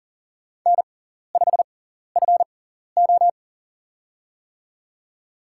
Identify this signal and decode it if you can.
Morse code, heard as a keyed tone, about 40 wpm, spaced as for 14 wpm, 700 Hz. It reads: N5FO
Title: USA Call Signs - 4 Characters - 6x Character Spacing - 40wpm